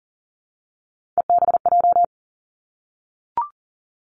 Morse code, heard as a keyed tone, about 41 wpm, 700 Hz. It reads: EBJ
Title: Three Challenging Letter-Number Combinations 40wpm